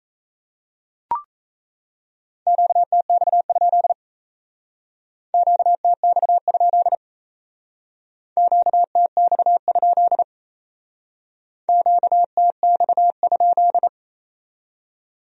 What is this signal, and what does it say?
Morse code, heard as a keyed tone, about 35 wpm, 700 Hz. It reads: QTX? QTX? QTX? QTX?
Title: Speed-Racing Q-Codes - 28wpm